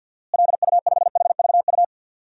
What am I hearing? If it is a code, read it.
Morse code, heard as a keyed tone, about 51 wpm, 700 Hz. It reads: LU5HVU
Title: DXCC-100 Call Signs - 6 Characters - Repeated 2x - 50wpm